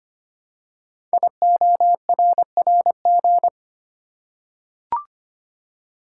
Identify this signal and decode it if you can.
Morse code, heard as a keyed tone, about 25 wpm, 700 Hz. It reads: IORRG
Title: Code Groups - 5 Letters - 25wpm